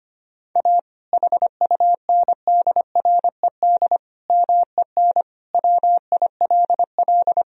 Morse code, heard as a keyed tone, about 25 wpm, 700 Hz. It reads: A HUNDRED MEN WILL
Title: Sentences from Top 1000 Words - Review Part 2 - 25wpm